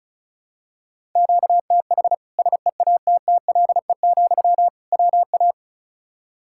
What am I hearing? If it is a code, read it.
Morse code, heard as a keyed tone, about 36 wpm, 700 Hz. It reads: QTH SEATTLE, WA
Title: Practice QSO - Exchange 1 - Beginning - 35wpm